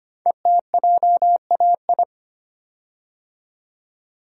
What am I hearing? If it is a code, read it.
Morse code, heard as a keyed tone, about 25 wpm, 700 Hz. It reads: ETJAI
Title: Code Groups - 5 Letters - 25wpm